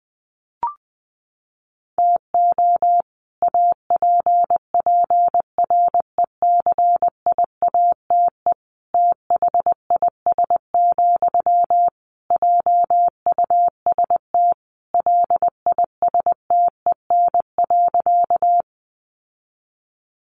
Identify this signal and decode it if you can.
Morse code, heard as a keyed tone, about 19 wpm, 700 Hz. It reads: TO APPRECIATE THIS, JUST LISTEN.
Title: Sentences from Top 1300 Words 20wpm